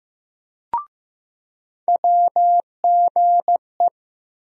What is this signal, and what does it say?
Morse code, heard as a keyed tone, about 15 wpm, 700 Hz. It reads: WGE
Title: USA Call Sign Postfixes - 3 Letters 15wpm